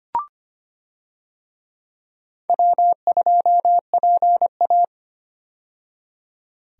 Morse code, heard as a keyed tone, about 25 wpm, 700 Hz. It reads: W2PA
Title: USA Call Signs - 4 Characters - 25wpm